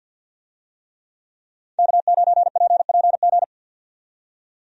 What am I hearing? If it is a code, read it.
Morse code, heard as a keyed tone, about 50 wpm, 700 Hz. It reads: K9PPG
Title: USA Call Signs - 5 Characters - 50wpm